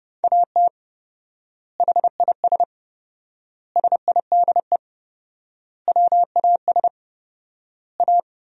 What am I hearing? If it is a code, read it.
Morse code, heard as a keyed tone, about 30 wpm, 700 Hz. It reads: AT HIS SIDE WAS A
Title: 4x Word-Spacing - Sentences from Top 500 Words - 30wpm